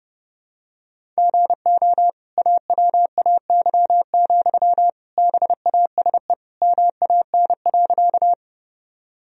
Morse code, heard as a keyed tone, about 32 wpm, 700 Hz. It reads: GO AWAY, BASE MAN.
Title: Sentences from Top 1000 Words - Review Part 1 - 30wpm